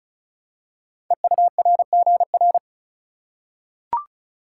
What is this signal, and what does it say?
Morse code, heard as a keyed tone, about 36 wpm, 700 Hz. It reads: EURGR
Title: Code Groups - 5 Letters - 35wpm